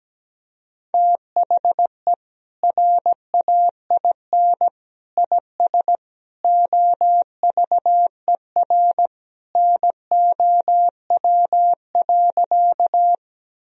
Morse code, heard as a keyed tone, about 17 wpm, 700 Hz. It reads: THE RAIN IS OVER NOW.
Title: Sentences from Top 1000 Words - Review Part 1 - 17wpm